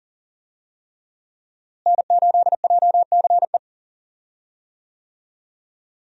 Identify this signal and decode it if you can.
Morse code, heard as a keyed tone, about 38 wpm, 700 Hz. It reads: N8JCE
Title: USA Call Signs - 5 Characters - Repeated 3x - 40wpm